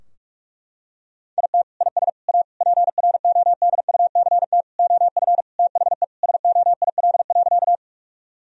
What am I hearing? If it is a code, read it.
Morse code, heard as a keyed tone, about 41 wpm, 700 Hz. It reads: IT IS A PRODUCT OF THE SOIL.
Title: Sentences from Top 1000 Words - Review Part 4 - 45wpm